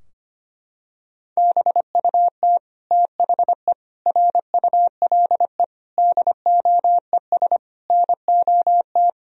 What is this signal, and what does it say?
Morse code, heard as a keyed tone, about 25 wpm, 700 Hz. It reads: BUT THE RULE DOES NOT
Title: Sentences from Top 1000 Words - Review Part 2 - 25wpm